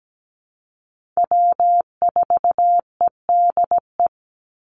Morse code, heard as a keyed tone, about 17 wpm, 700 Hz. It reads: W4EDE
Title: Call Signs Repeated 2x - 17wpm